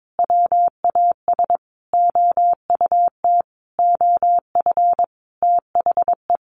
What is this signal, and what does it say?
Morse code, heard as a keyed tone, about 22 wpm, 700 Hz. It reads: WAS OUT OF THE
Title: Sentences from Top 600 Words 22wpm